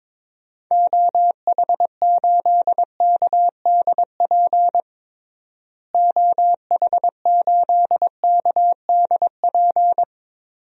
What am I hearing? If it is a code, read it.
Morse code, heard as a keyed tone, about 22 wpm, 700 Hz. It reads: OH8KDP OH8KDP
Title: DXCC-100 Call Signs - 6 Characters - Repeated 2x - 22wpm